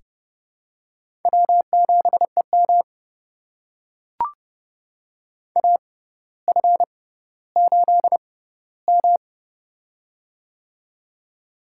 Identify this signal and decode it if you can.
Morse code, heard as a keyed tone, about 30 wpm, spaced as for 15 wpm, 700 Hz. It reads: W7EM AF8M
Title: USA Call Signs - 4 Characters - 6x Character Spacing - 30wpm